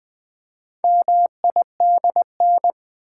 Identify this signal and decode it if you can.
Morse code, heard as a keyed tone, about 20 wpm, 700 Hz. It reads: MIDN
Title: Build A Word - 4 Letters At A Time - 8-Letter Words - 20wpm